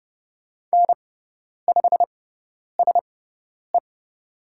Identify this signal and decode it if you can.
Morse code, heard as a keyed tone, about 30 wpm, spaced as for 10 wpm, 700 Hz. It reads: N5SE
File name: Lesson 16g - Letter c - Call Signs 30wpm